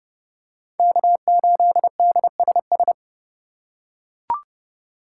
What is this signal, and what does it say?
Morse code, heard as a keyed tone, about 30 wpm, 700 Hz. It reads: K8DSS
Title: K1USN SST QSO Practice - 30wpm